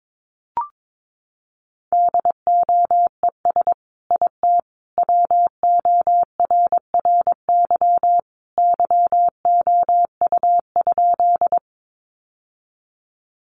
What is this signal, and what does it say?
Morse code, heard as a keyed tone, about 22 wpm, 700 Hz. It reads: DOES IT WORRY YOU?
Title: Sentences from Top 1100 Words 22wpm